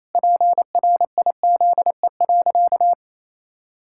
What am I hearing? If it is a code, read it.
Morse code, heard as a keyed tone, about 28 wpm, 700 Hz. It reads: PRIZE.
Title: Pangram Challenge v2 - 28wpm